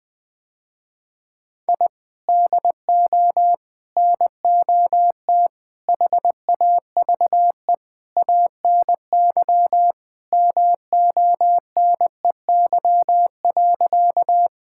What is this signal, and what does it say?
Morse code, heard as a keyed tone, about 20 wpm, 700 Hz. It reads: I DO NOT HAVE ANY MONEY.
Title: Sentences with 5-Word Phrases - Spoken Beginning - 20wpm